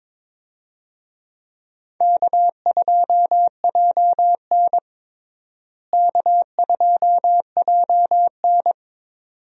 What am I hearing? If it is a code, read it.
Morse code, heard as a keyed tone, about 22 wpm, 700 Hz. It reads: K2JN K2JN
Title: USA Call Signs - 4 Characters - Repeated 2x - 22wpm